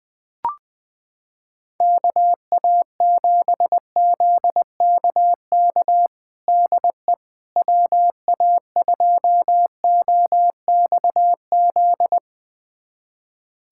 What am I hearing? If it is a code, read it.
Morse code, heard as a keyed tone, about 20 wpm, 700 Hz. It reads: KA7ZKK DE WA2OXZ
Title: USA Callsign DE Callsign - 6 Characters - 20wpm